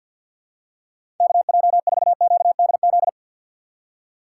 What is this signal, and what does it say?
Morse code, heard as a keyed tone, about 51 wpm, 690 Hz. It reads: KJ4QDZ
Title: USA Call Signs - 6 Characters - 3x Character Spacing - 50wpm